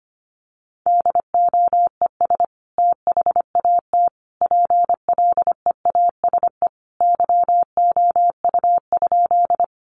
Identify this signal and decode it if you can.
Morse code, heard as a keyed tone, about 25 wpm, 700 Hz. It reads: DOES THAT PLEASE YOU?
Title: Sentences from Top 1500 Words 25wpm